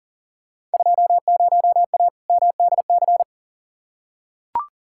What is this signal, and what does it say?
Morse code, heard as a keyed tone, about 39 wpm, 700 Hz. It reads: 20A MDC
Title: Field Day Exchanges Encore - 40wpm